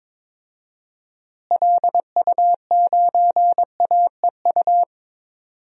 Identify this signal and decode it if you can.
Morse code, heard as a keyed tone, about 22 wpm, 700 Hz. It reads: LU9AEU
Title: DXCC-100 Call Signs - 6 Characters - Repeated 2x - 22wpm